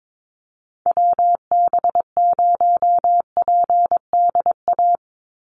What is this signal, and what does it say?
Morse code, heard as a keyed tone, about 22 wpm, 700 Hz. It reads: WB0PDA
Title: USA Call Signs 22wpm